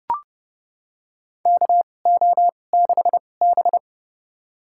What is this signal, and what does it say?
Morse code, heard as a keyed tone, about 30 wpm, spaced as for 22 wpm, 700 Hz. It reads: KO6B
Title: USA Call Signs - 2x Character Spacing - 30wpm